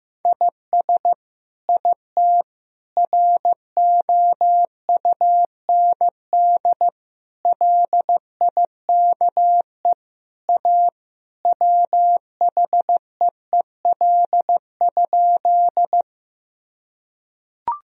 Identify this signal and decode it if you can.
Morse code, heard as a keyed tone, about 15 wpm, 700 Hz. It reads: IS IT ROUND LIKE A WHEEL?
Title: Sentences from Top 1000 Words - Review Part 3 - 15wpm